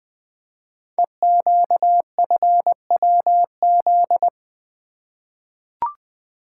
Morse code, heard as a keyed tone, about 20 wpm, 700 Hz. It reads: EQFWZ
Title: Code Groups - 5 Letter:Numbers - 20wpm